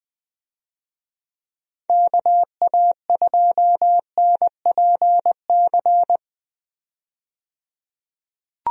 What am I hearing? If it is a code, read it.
Morse code, heard as a keyed tone, about 20 wpm, 700 Hz. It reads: KA2NPC
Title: USA Call Signs - 6 Characters - 20wpm